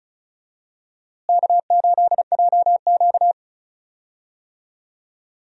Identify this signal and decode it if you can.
Morse code, heard as a keyed tone, about 36 wpm, 700 Hz. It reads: K8JQ
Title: USA Call Signs - 4 Characters - Repeated 2x - 35wpm